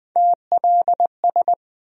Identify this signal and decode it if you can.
Morse code, heard as a keyed tone, about 20 wpm, 700 Hz. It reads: TLS
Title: USA Call Sign Postfixes - 3 Letters 20wpm